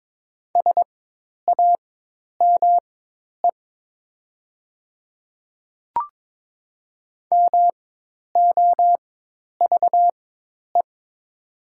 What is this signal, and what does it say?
Morse code, heard as a keyed tone, about 22 wpm, spaced as for 10 wpm, 700 Hz. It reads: SAME MOVE
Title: 4x Character Spacing - Top 200 Words - 22wpm